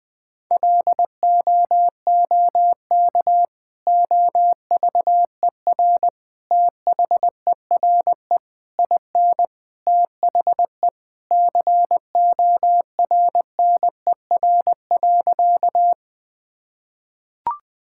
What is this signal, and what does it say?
Morse code, heard as a keyed tone, about 20 wpm, 700 Hz. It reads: LOOK OVER THERE IN THE CORNER.
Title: Sentences from Top 1600 Words 20wpm